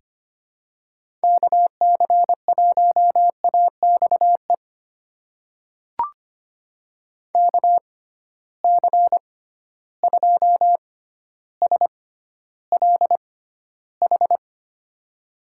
Morse code, heard as a keyed tone, about 25 wpm, spaced as for 13 wpm, 700 Hz. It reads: KC1AXE KC2SLH
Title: USA Call Signs - 6 Characters - 6x Character Spacing - 25wpm